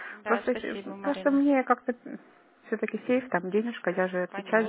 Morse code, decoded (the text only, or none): none